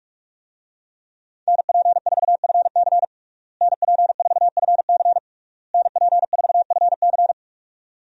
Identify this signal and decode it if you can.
Morse code, heard as a keyed tone, about 46 wpm, 700 Hz. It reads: NP4FC NP4FC NP4FC
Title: USA Call Signs - 5 Characters - Repeated 3x - 45wpm